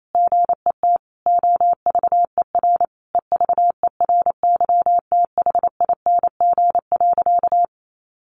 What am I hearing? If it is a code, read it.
Morse code, heard as a keyed tone, about 28 wpm, 690 Hz. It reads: GET OVER EVERYTHING.
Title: Sentences from Top 900 Words 28wpm